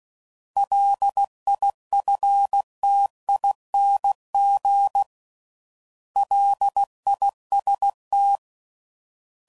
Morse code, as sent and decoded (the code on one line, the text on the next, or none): .-.. .. ..-. - .. -. --. / .-.. .. ... -
LIFTING LIST